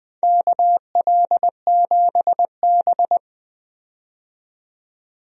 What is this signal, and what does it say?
Morse code, heard as a keyed tone, about 20 wpm, 700 Hz. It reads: KL7B
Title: DXCC-100 Call Signs - 4 Characters - 20wpm